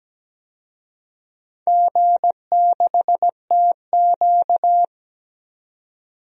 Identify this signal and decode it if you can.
Morse code, heard as a keyed tone, about 17 wpm, 700 Hz. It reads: G6TQ
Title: DXCC-100 Call Signs - 4 Characters - 17wpm